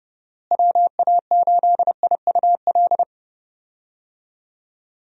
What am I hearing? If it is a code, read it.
Morse code, heard as a keyed tone, about 30 wpm, 700 Hz. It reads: WA8IUL
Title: USA Call Signs - 6 Characters - Repeated 3x - 30wpm